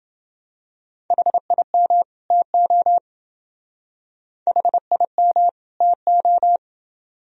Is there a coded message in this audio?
Yes: Morse code from 1.1 to 6.6 s, heard as a keyed tone, about 30 wpm, 700 Hz. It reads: HIM TO HIM TO